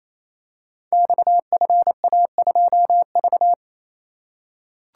Morse code, heard as a keyed tone, about 27 wpm, 700 Hz. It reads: XFA2V
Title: Code Groups - 5 Letter:Numbers - 28wpm